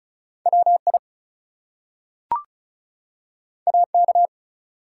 Morse code, heard as a keyed tone, about 36 wpm, 700 Hz. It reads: WI AK
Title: US States and Canadian Provinces 35wpm